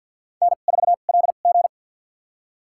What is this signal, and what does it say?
Morse code, heard as a keyed tone, about 52 wpm, spaced as for 36 wpm, 690 Hz. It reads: N4LG
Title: USA Call Signs - 2x Character Spacing - 50wpm